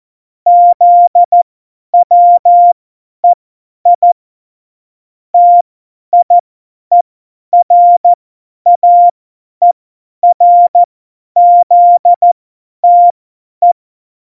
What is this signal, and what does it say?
Morse code, heard as a keyed tone, about 14 wpm, spaced as for 10 wpm, 700 Hz. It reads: ZWEI TIERAERZTE
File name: Morsekurs / Witze / Witz 18. Dezember, 50 F70